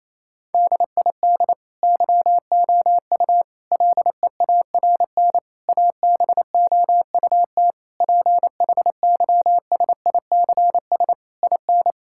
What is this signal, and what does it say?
Morse code, heard as a keyed tone, about 28 wpm, 700 Hz. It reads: DID YOU LEARN ABOUT PHYSICS IN
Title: Sentences from Top 1000 Words - Review Part 5 - 28wpm